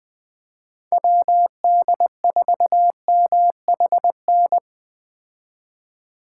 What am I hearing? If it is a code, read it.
Morse code, heard as a keyed tone, about 20 wpm, 700 Hz. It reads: WD4MHN